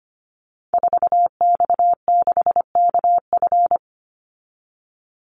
Morse code, heard as a keyed tone, about 25 wpm, 700 Hz. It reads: 4X6KF
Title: Call Signs Repeated 3x - 25wpm